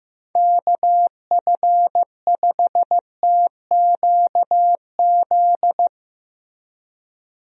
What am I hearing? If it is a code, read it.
Morse code, heard as a keyed tone, about 16 wpm, 700 Hz. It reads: KF5TQZ
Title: USA Call Signs - 6 Characters - Repeated 2x - 15wpm